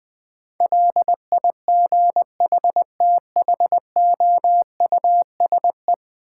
Morse code, heard as a keyed tone, about 21 wpm, 700 Hz. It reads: LIGHTHOUSE
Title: Difficult Words - Speed-Racing with Hints - 20wpm